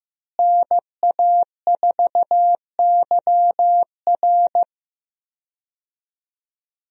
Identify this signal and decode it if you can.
Morse code, heard as a keyed tone, about 15 wpm, 700 Hz. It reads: NA4YR
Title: USA Call Signs - 5 Characters - Repeated 3x - 15wpm